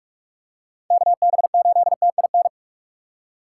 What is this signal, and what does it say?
Morse code, heard as a keyed tone, about 44 wpm, 700 Hz. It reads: KB8TIN
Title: USA Call Signs - 4x Character Spacing - 45wpm